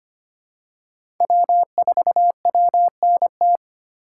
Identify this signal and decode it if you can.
Morse code, heard as a keyed tone, about 25 wpm, 700 Hz. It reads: W4WNT
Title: K1USN SST QSO Practice - 25wpm